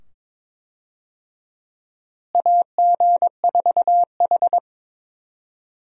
Morse code, heard as a keyed tone, about 22 wpm, 700 Hz. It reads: AG4H
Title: USA Call Signs - 4 Characters - Repeated 2x - 22wpm